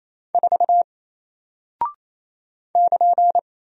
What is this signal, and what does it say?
Morse code, heard as a keyed tone, about 29 wpm, 700 Hz. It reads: 4 <KN>
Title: Single Character (Letter:Number:Punctuation:Prosign) Alt 28wpm